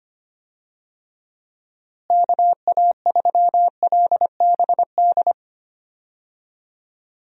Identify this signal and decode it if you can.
Morse code, heard as a keyed tone, about 25 wpm, 700 Hz. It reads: KA3LBD
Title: USA Call Signs - 6 Characters - Repeated 2x - 25wpm